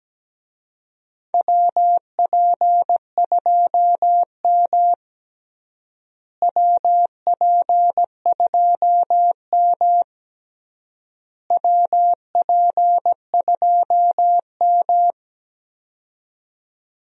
Morse code, heard as a keyed tone, about 17 wpm, 700 Hz. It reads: WP2M WP2M WP2M
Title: USA Call Signs - 4 Characters - Repeated 3x - 17wpm